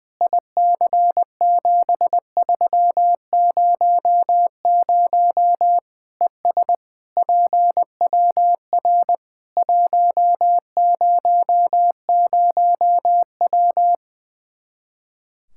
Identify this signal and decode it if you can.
Morse code, heard as a keyed tone, about 20 wpm, 700 Hz. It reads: IC7300 ES PWR 100W
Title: Practice QSO - Exchange 2 - Info - 20wpm